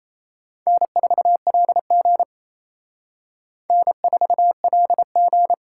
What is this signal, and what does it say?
Morse code, heard as a keyed tone, about 34 wpm, 700 Hz. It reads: N4LG N4LG
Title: Speed Racing - USA 1x2 Call Signs 28wpm